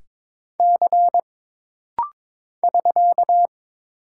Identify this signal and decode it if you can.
Morse code, heard as a keyed tone, about 22 wpm, 700 Hz. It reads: C <SK>
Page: This rapid-fire timing is geared to push your ICR (Instant Character Recognition) skills.